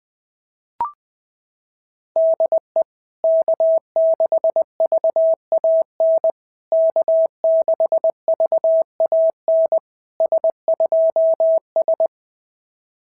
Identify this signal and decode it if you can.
Morse code, heard as a keyed tone, about 20 wpm, 650 Hz. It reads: DE K6VAN K6VAN S2S
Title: SOTA QSO Practice 20wpm